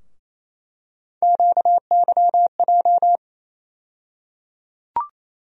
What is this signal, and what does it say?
Morse code, heard as a keyed tone, about 28 wpm, 700 Hz. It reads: QYJ
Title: USA Call Sign Postfixes - 3 Letters 28wpm